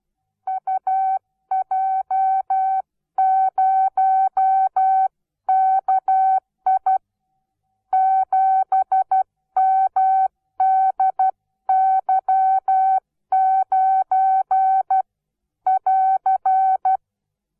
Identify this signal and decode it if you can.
Morse code, heard as a keyed tone, about 12 wpm, 760 Hz. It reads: UJ0KI 7MDY9<AR>
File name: Onlinelehrgang / morsen / DJ4UF Morsen17